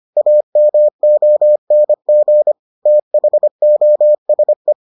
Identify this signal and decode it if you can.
Morse code, heard as a keyed tone, about 25 wpm, 600 Hz. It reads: AMONG THOSE